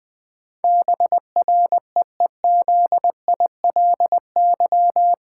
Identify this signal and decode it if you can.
Morse code, heard as a keyed tone, about 20 wpm, 700 Hz. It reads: BREEZILY
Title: Pangram Challenge v2 - 20wpm